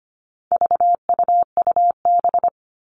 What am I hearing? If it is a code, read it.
Morse code, heard as a keyed tone, about 25 wpm, 700 Hz. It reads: VUUB
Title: All Cut Numbers - 4-Digit 25wpm